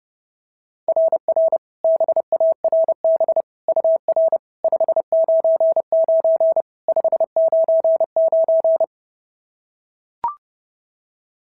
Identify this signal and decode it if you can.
Morse code, heard as a keyed tone, about 32 wpm, 690 Hz. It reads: RR BARB UR 599 599
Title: SOTA QSO Practice 30wpm